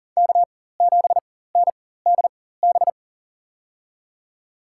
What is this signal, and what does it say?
Morse code, heard as a keyed tone, about 40 wpm, spaced as for 19 wpm, 700 Hz. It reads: K7NDB
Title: USA Call Signs - 5 Characters - 4x Character Spacing - 40wpm